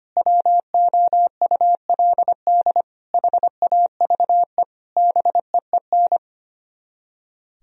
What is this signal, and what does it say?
Morse code, heard as a keyed tone, about 31 wpm, 700 Hz. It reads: WOULD HAVE BEEN
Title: Speed-Racing - Sets of 4 Words 25wpm